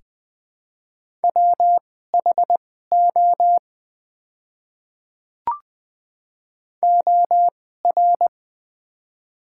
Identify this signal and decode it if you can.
Morse code, heard as a keyed tone, about 20 wpm, spaced as for 14 wpm, 700 Hz. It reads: WHO OR
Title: 2x Character Spacing - Top 100 Words - 20wpm